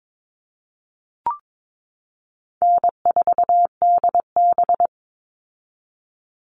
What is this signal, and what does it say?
Morse code, heard as a keyed tone, about 22 wpm, 700 Hz. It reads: N4DB
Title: All Cut Numbers - 4-Digit 22wpm